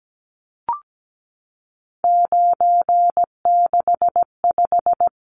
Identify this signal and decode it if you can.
Morse code, heard as a keyed tone, about 17 wpm, 700 Hz. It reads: 965
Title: Numbers - 3-Digit 17wpm